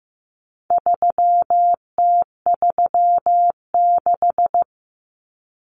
Morse code, heard as a keyed tone, about 15 wpm, 700 Hz. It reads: 3T36
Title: Common Cut Numbers - 4-Digit 15wpm